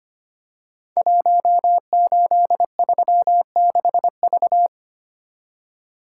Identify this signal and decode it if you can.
Morse code, heard as a keyed tone, about 25 wpm, 700 Hz. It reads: 1836V
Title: Code Groups - 5 Letter:Numbers - 25wpm